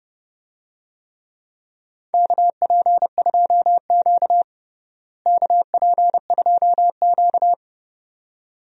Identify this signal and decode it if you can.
Morse code, heard as a keyed tone, about 30 wpm, 700 Hz. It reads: KP2Q KP2Q
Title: DXCC-100 Call Signs - 4 Characters - Repeated 2x - 30wpm